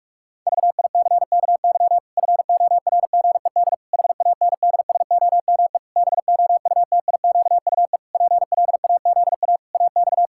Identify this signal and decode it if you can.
Morse code, heard as a keyed tone, about 46 wpm, 700 Hz. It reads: VICKY FORGED HANDSOME BOUTIQUE PLAZA AX
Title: Pangram Challenge v2 - 45wpm